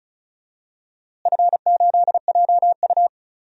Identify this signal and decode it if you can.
Morse code, heard as a keyed tone, about 36 wpm, 700 Hz. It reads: F8JU
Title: DXCC-100 Call Signs - 4 Characters - Repeated 2x - 35wpm